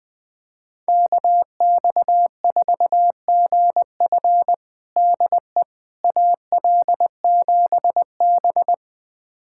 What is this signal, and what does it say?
Morse code, heard as a keyed tone, about 20 wpm, 700 Hz. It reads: KX4GF DE AL7B
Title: USA Callsign DE Callsign 20wpm